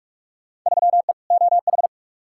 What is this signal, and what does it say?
Morse code, heard as a keyed tone, about 46 wpm, 700 Hz. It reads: 3E OH
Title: Field Day Exchanges - 45wpm